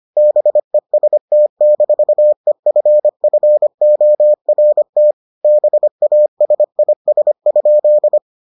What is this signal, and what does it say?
Morse code, heard as a keyed tone, about 25 wpm, 600 Hz. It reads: BEST-EFFORT BASIS?